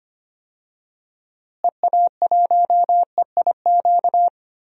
Morse code, heard as a keyed tone, about 25 wpm, 700 Hz. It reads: EA1EIQ
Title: DXCC-100 Call Signs - 6 Characters - Repeated 3x - 25wpm